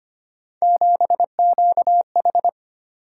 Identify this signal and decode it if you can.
Morse code, heard as a keyed tone, about 25 wpm, 700 Hz. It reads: 7QH
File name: Three Letter-Number Combinations - 25wpm